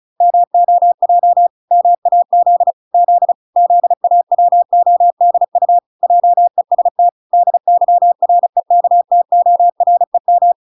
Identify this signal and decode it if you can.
Morse code, heard as a keyed tone, about 34 wpm, 700 Hz. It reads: MOJ MAZ Z ZAWODU JEST DYREKTOREM